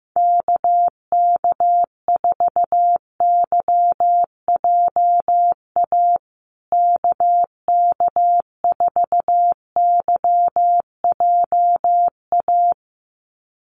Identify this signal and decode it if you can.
Morse code, heard as a keyed tone, about 15 wpm, 700 Hz. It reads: KK4YJA KK4YJA
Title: USA Call Signs Repeated 2x 15wpm